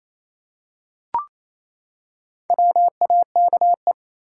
Morse code, heard as a keyed tone, about 27 wpm, 700 Hz. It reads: WAKE
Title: Four Letter Words - 28wpm